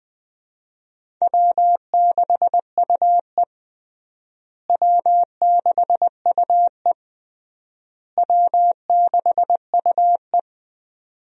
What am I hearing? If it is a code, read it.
Morse code, heard as a keyed tone, about 20 wpm, 700 Hz. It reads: W6UE W6UE W6UE
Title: USA Call Signs - 4 Characters - Repeated 3x - 20wpm